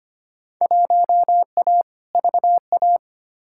Morse code, heard as a keyed tone, about 25 wpm, 700 Hz. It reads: 1A VA
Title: Field Day Exchanges - 25wpm